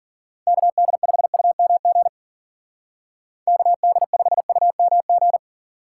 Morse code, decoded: KD5UMG KD5UMG

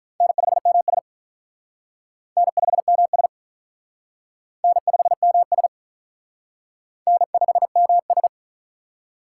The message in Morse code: N5MS N5MS N5MS N5MS

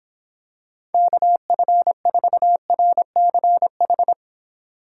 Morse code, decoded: KF4RCH

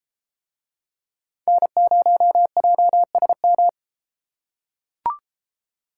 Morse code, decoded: N0JSM

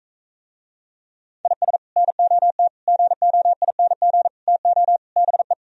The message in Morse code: IS NOT GOING TO BE

